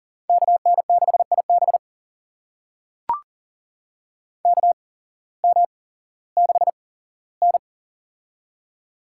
Morse code, heard as a keyed tone, about 37 wpm, spaced as for 18 wpm, 700 Hz. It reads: KN6IB KM6N
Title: USA Call Signs - 8x Character Spacing - 40wpm